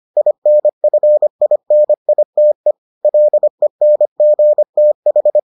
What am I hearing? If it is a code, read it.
Morse code, heard as a keyed tone, about 25 wpm, 600 Hz. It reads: INFINITE LENGTH